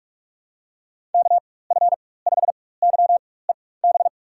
Morse code, heard as a keyed tone, about 44 wpm, spaced as for 21 wpm, 700 Hz. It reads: KF5YEB
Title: USA Call Signs - 6 Characters - 4x Character Spacing - 45wpm